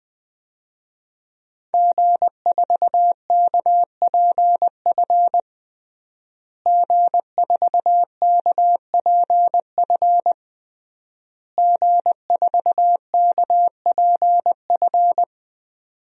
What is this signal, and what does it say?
Morse code, heard as a keyed tone, about 20 wpm, 700 Hz. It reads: G4KPF G4KPF G4KPF
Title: DXCC-100 Call Signs - 5 Characters - Repeated 3x - 20wpm